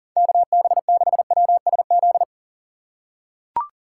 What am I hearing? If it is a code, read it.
Morse code, heard as a keyed tone, about 40 wpm, 700 Hz. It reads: KB6WSZ
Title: USA Call Signs - 6 Characters - 2x Character Spacing - 40wpm